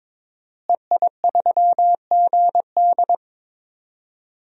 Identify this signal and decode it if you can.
Morse code, heard as a keyed tone, about 21 wpm, 700 Hz. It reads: EI3GD
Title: DXCC-100 Call Signs - 5 Characters - Repeated 2x - 22wpm